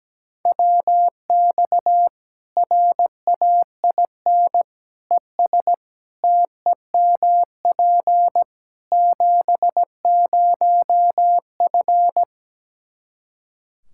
Morse code, heard as a keyed tone, about 17 wpm, 700 Hz. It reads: WX RAIN ES TEMP 70F
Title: Practice QSO - Exchange 2 - Info - 17wpm